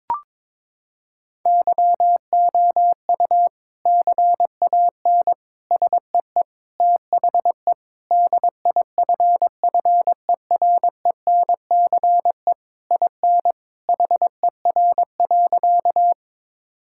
Morse code, heard as a keyed tone, about 22 wpm, 700 Hz. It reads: YOU CAN SEE THE DIFFERENCE IN HER.
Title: Sentences from Top 1000 Words - Review Part 5 - 22wpm